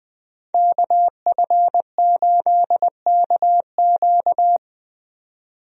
Morse code, heard as a keyed tone, about 20 wpm, 700 Hz. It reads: KF8KQ
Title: USA Call Signs - 5 Characters - 20wpm